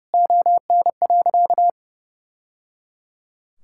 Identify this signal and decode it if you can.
Morse code, heard as a keyed tone, about 30 wpm, 700 Hz. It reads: ON.
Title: Sentences from Top 1000 Words - Review Part 5 - 30wpm